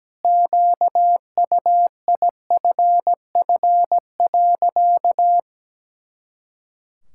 Morse code, heard as a keyed tone, about 17 wpm, 700 Hz. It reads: QUIFF.
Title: Pangram Challenge v2 - 17wpm